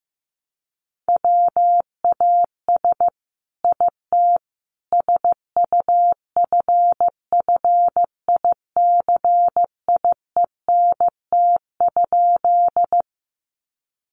Morse code, heard as a keyed tone, about 15 wpm, 700 Hz. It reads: WAS IT SUFFICIENT?
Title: Sentences from Top 1400 Words 15wpm